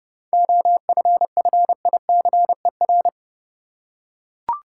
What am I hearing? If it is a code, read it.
Morse code, heard as a keyed tone, about 30 wpm, 700 Hz. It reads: OFFICER
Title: Seven Letter Words - 30wpm